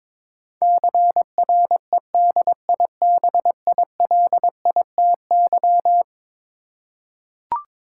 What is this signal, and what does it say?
Morse code, heard as a keyed tone, about 22 wpm, 700 Hz. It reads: CREDIBILITY
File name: practice / 22wpm / Eleven Letter Words - 22wpm